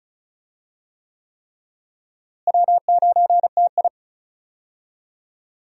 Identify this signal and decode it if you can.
Morse code, heard as a keyed tone, about 36 wpm, 700 Hz. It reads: W9TI
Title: USA Call Signs - 4 Characters - Repeated 3x - 35wpm